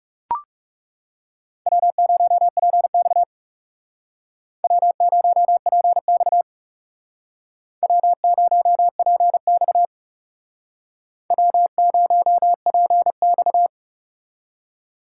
Morse code, heard as a keyed tone, about 37 wpm, 700 Hz. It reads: W0PX W0PX W0PX W0PX
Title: Speed Racing - USA 1x2 Call Signs 30wpm